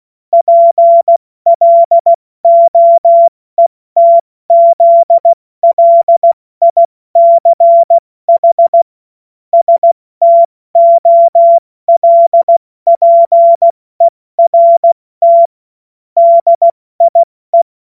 Morse code, heard as a keyed tone, about 16 wpm, 680 Hz. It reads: PLOETZLICH STOLPERT DIE
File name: Morsekurs / Witze / Witz 17. August 2021, 70 F80